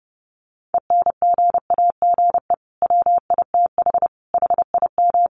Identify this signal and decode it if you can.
Morse code, heard as a keyed tone, about 30 wpm, 700 Hz. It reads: ENGAGE WITH HIM
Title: Sentences from Top 1500 Words 30wpm